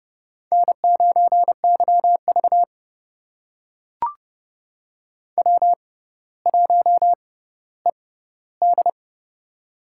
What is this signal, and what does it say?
Morse code, heard as a keyed tone, about 30 wpm, spaced as for 15 wpm, 700 Hz. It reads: N9YV W1ED
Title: USA Call Signs - 4 Characters - 6x Character Spacing - 30wpm